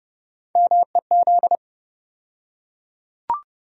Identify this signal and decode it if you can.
Morse code, heard as a keyed tone, about 30 wpm, 700 Hz. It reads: MEZ
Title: USA Call Sign Postfixes - 3 Letters 30wpm